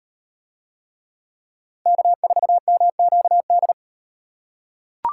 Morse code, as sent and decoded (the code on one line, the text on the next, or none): -.- ....- -- --.- -..
K4MQD